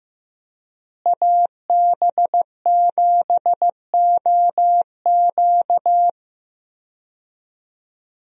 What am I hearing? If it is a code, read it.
Morse code, heard as a keyed tone, about 15 wpm, 700 Hz. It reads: AB7OQ
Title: USA Call Signs - 5 Characters - Repeated 3x - 15wpm